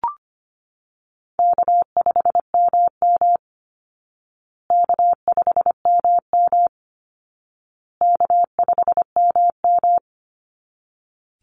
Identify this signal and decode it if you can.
Morse code, heard as a keyed tone, about 25 wpm, 700 Hz. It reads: K5MM K5MM K5MM
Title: Easy Call Signs Repeated 3x - 25wpm